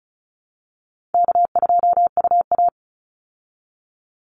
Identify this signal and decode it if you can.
Morse code, heard as a keyed tone, about 34 wpm, 700 Hz. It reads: K2UA
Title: Call Signs Repeated 3x - 35wpm